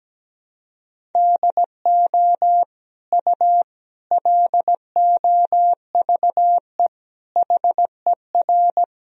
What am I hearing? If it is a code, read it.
Morse code, heard as a keyed tone, about 17 wpm, 700 Hz. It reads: DO U LOVE HER
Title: Sentences from Top 400 Words and Abbreviations 17wpm